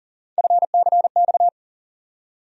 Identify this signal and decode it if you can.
Morse code, heard as a keyed tone, about 40 wpm, 700 Hz. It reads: FCX
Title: USA Call Sign Postfixes - 3 Letters 40wpm